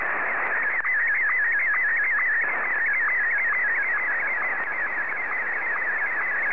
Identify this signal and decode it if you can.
SSTV (slow-scan television), Martin 1.